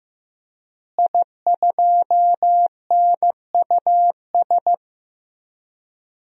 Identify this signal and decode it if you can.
Morse code, heard as a keyed tone, about 15 wpm, 700 Hz. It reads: I2NUS
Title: DXCC-100 Call Signs - 5 Characters - Repeated 2x - 15wpm